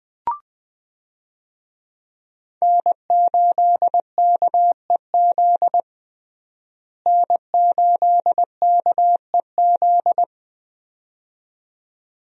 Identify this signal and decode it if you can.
Morse code, heard as a keyed tone, about 20 wpm, 700 Hz. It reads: N8KEZ N8KEZ
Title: USA Call Signs - 5 Characters - Repeated 2x - 20wpm